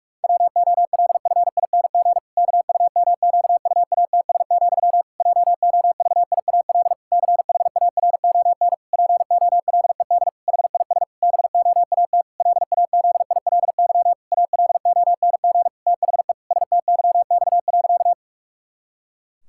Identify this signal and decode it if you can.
Morse code, heard as a keyed tone, about 46 wpm, 700 Hz. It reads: WOLFING KUMQUATS, JOVIAL CHARON POLED HIS BOAT LAZILY ALONG THE STYX.
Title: Pangram Challenge v2 - 45wpm